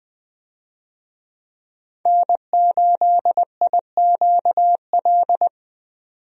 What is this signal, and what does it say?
Morse code, heard as a keyed tone, about 20 wpm, 700 Hz. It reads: N8IQL
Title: USA Call Signs - 5 Characters - 20wpm